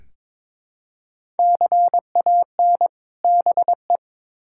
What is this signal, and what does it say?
Morse code, heard as a keyed tone, about 21 wpm, 700 Hz. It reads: CAN BE
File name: Build A Phrase - 2-Word Phrases - 22wpm